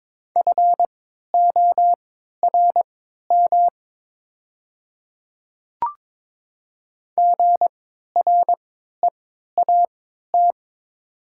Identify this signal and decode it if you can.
Morse code, heard as a keyed tone, about 23 wpm, spaced as for 13 wpm, 700 Hz. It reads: FORM GREAT
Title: 3x Character Spacing - Top 200 Words - 22wpm